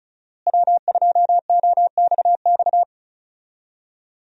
Morse code, heard as a keyed tone, about 34 wpm, 700 Hz. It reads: W2OXX
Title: USA Call Signs - 5 Characters - Repeated 2x - 35wpm